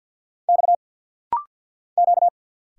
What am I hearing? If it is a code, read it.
Morse code, heard as a keyed tone, about 50 wpm, 700 Hz. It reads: X Q
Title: Single Letter-Number - Rapid-Fire 50wpm